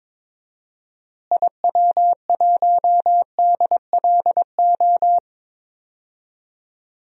Morse code, heard as a keyed tone, about 22 wpm, 700 Hz. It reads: IW1DLO
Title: DXCC-100 Call Signs - 6 Characters - Repeated 2x - 22wpm